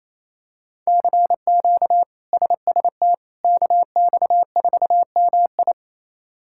Morse code, heard as a keyed tone, about 27 wpm, 700 Hz. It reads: CQ SST KX4MI
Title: K1USN SST QSO Practice - 28wpm